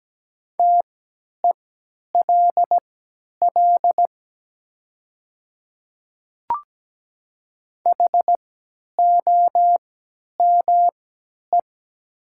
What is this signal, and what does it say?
Morse code, heard as a keyed tone, about 17 wpm, spaced as for 10 wpm, 700 Hz. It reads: TELL HOME
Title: 3x Character Spacing - Top 300 Words - 17wpm